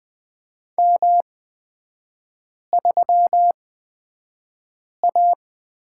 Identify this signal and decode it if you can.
Morse code, heard as a keyed tone, about 20 wpm, spaced as for 5 wpm, 700 Hz. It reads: M3A
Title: 224 Lesson 36e - BT - Three Characters 20wpm